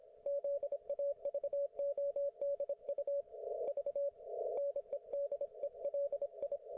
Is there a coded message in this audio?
Yes: Morse code from 0.3 to 6.6 s, heard as a keyed tone, about 27 wpm, 560 Hz. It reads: ZAVODU V NEDELI